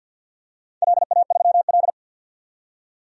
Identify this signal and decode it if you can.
Morse code, heard as a keyed tone, about 47 wpm, 700 Hz. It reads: LA3L
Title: DXCC-100 Call Signs - 4 Characters - 50wpm